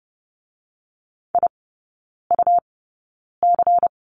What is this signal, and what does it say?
Morse code, heard as a keyed tone, about 30 wpm, spaced as for 9 wpm, 690 Hz. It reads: IUC
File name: Lesson 17e - Letter u - Three Characters 30wpm